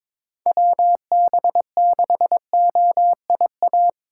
Morse code, heard as a keyed tone, about 22 wpm, 700 Hz. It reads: WB6OIA
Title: USA Call Signs - 6 Characters - 2x Character Spacing - 22wpm